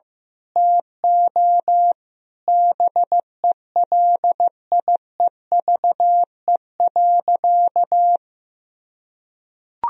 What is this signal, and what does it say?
Morse code, heard as a keyed tone, about 15 wpm, 700 Hz. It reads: TO BELIEVE.
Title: Sentences from Top 1000 Words - Review Part 4 - 15wpm